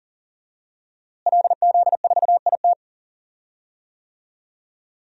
Morse code, heard as a keyed tone, about 40 wpm, 700 Hz. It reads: LZ4IT
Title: DXCC-100 Call Signs - 5 Characters - Repeated 2x - 40wpm